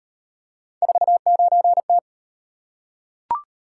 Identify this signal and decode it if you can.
Morse code, heard as a keyed tone, about 36 wpm, 700 Hz. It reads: 49T